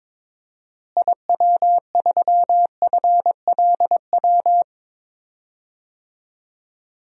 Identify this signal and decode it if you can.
Morse code, heard as a keyed tone, about 22 wpm, 700 Hz. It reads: IW3FLW
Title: DXCC-100 Call Signs - 6 Characters - Repeated 3x - 22wpm